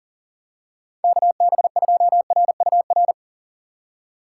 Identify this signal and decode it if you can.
Morse code, heard as a keyed tone, about 40 wpm, 700 Hz. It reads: KB2RUR
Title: USA Call Signs - 4x Character Spacing - 40wpm